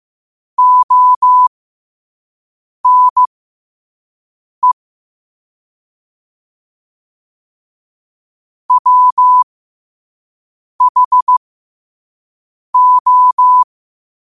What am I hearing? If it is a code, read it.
Morse code, heard as a keyed tone, about 15 wpm, spaced as for 5 wpm, 1000 Hz. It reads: ONE WHO
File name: qotdfiles / 5wpm